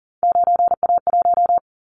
Morse code, heard as a keyed tone, about 42 wpm, 700 Hz. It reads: 9A1
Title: Callsign Prefixes for Top 100 DXCC - 40wpm